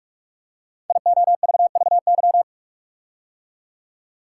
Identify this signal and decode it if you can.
Morse code, heard as a keyed tone, about 44 wpm, 700 Hz. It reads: IOVVY
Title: Code Groups - 5 Letter:Numbers - 45wpm